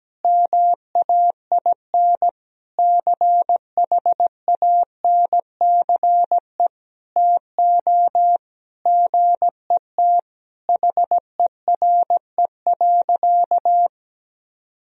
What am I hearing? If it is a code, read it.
Morse code, heard as a keyed tone, about 17 wpm, 700 Hz. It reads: MAIN CHANCE TO GET HERE.
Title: Sentences from Top 1000 Words - Review Part 2 - 17wpm